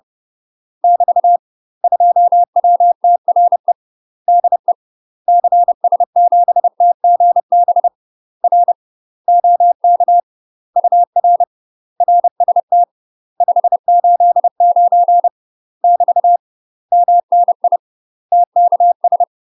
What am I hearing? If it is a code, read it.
Morse code, heard as a keyed tone, about 30 wpm, 700 Hz. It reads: <BT> 2WTRE DE CS7TGB R OK UR RST 589 <BT> MNI TKS
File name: UK MASTER QSOs TRAINING / 30WPM / UK MASTER QSOs 0033